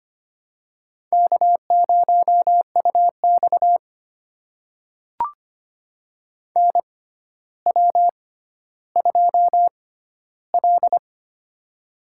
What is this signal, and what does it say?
Morse code, heard as a keyed tone, about 25 wpm, spaced as for 13 wpm, 700 Hz. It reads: K0UX NW2L
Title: USA Call Signs - 4 Characters - 6x Character Spacing - 25wpm